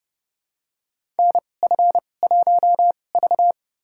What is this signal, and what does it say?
Morse code, heard as a keyed tone, about 30 wpm, spaced as for 22 wpm, 700 Hz. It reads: NF1V
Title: USA Call Signs - 4 Characters - 2x Character Spacing - 30wpm